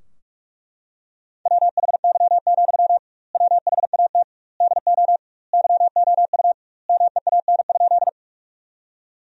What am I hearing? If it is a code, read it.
Morse code, heard as a keyed tone, about 41 wpm, 700 Hz. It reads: WHY, WHAT DO YOU MEAN?
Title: 2x Word-Spacing - Sentences from Top 200 Words - 45wpm